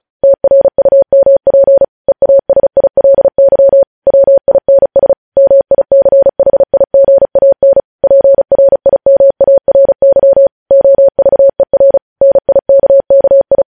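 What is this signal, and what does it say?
Morse code, heard as a keyed tone, about 36 wpm, 550 Hz. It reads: TRUMP EASILY WINS MICHIGAN PRIMARY OVER NIKKI
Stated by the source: Morse news headlines practice at 35 words per minute.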